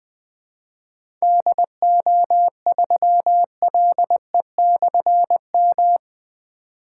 Morse code, heard as a keyed tone, about 20 wpm, 700 Hz. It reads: DO3LE/M
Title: Difficult Call Signs 20wpm